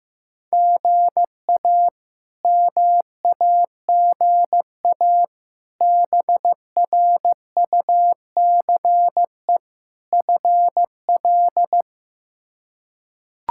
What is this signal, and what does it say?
Morse code, heard as a keyed tone, about 15 wpm, 700 Hz. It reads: GA MAGA BRUCE FL
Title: K1USN SST QSO Practice - 15wpm